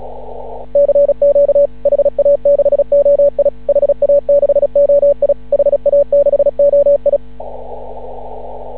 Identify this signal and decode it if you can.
Morse code, heard as a keyed tone, about 36 wpm, 570 Hz. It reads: CQ HA6OI HA6OI HA6OI
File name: CW / ha6oi 2